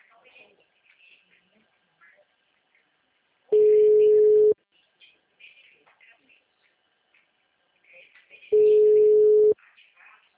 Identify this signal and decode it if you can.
Ringback tone (Europe, 425 Hz), with speech.
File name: out- -2011